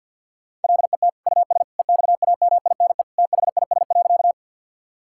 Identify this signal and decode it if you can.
Morse code, heard as a keyed tone, about 48 wpm, 700 Hz. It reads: LET US EXAMINE THIS.
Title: Sentences from Top 1600 Words 50wpm